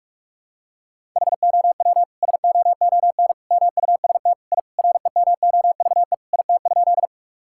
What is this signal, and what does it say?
Morse code, heard as a keyed tone, about 46 wpm, 700 Hz. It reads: HOW SOON MUST I REMOVE IT?
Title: Sentences from Top 1000 Words - Review Part 5 - 45wpm